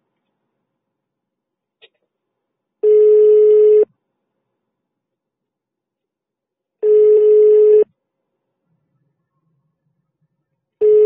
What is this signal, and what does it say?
Ringback tone (425 Hz, 1 s on, 3 s off).